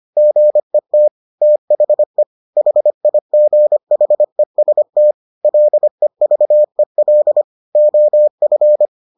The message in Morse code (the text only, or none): GET THE HIGHEST LEVEL OF